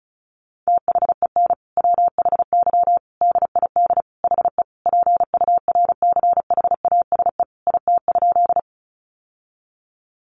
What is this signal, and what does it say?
Morse code, heard as a keyed tone, about 34 wpm, 700 Hz. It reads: THEN WHY DID HE PURCHASE IT?
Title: Sentences from Top 800 Words 35wpm